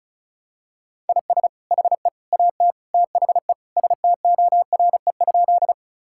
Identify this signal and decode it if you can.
Morse code, heard as a keyed tone, about 36 wpm, 700 Hz. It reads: IS HE AT THE STORE?
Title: Sentences from Top 1000 Words - Review Part 5 - 35wpm